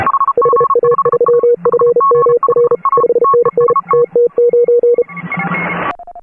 Morse code, heard as a keyed tone, about 32 wpm, 480 Hz. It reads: HA3FMR 5NN TT9